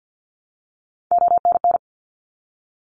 Morse code, heard as a keyed tone, about 50 wpm, 700 Hz. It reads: ONN
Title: ARRL Sections 50wpm